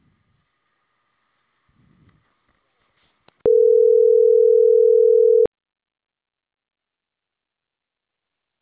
Ringback tone (North America), 440 + 480 Hz.